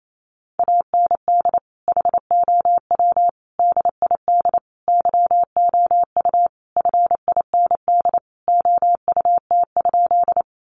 Morse code, heard as a keyed tone, about 28 wpm, 700 Hz. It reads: AND HOW DID YOU FIND OUT?
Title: Sentences from Top 100 Words 28wpm